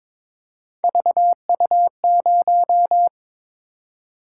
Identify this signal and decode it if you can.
Morse code, heard as a keyed tone, about 22 wpm, 700 Hz. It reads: VU0
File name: practice / 22wpm / Three Letter-Number Combinations - 22wpm